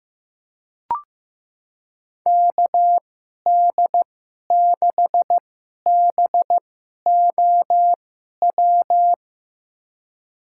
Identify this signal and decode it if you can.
Morse code, heard as a keyed tone, about 15 wpm, spaced as for 11 wpm, 700 Hz. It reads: KD6BOW
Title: USA Call Signs - 6 Characters - 2x Character Spacing - 15wpm